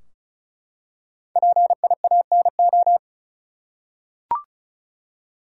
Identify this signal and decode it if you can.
Morse code, heard as a keyed tone, about 34 wpm, 700 Hz. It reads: PIANO
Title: Five Letter Words - 35wpm